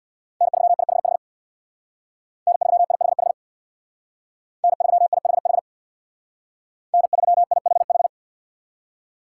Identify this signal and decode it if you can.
Morse code, heard as a keyed tone, about 64 wpm, 700 Hz. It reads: N3IHH N3IHH N3IHH N3IHH